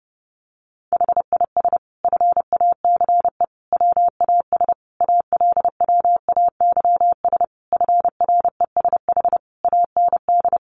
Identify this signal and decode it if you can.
Morse code, heard as a keyed tone, about 30 wpm, 700 Hz. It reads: HIS FACE WAS ALWAYS FRESH AND
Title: Sentences from Top 1200 Words 30wpm